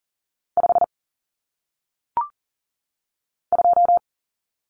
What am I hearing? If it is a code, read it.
Morse code, heard as a keyed tone, about 39 wpm, 700 Hz. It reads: 5 2
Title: Numbers - 1-Digit 40wpm